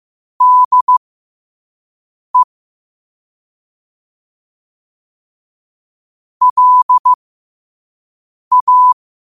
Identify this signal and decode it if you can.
Morse code, heard as a keyed tone, about 15 wpm, spaced as for 5 wpm, 1000 Hz. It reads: DE LA
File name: qotdfiles / 5wpm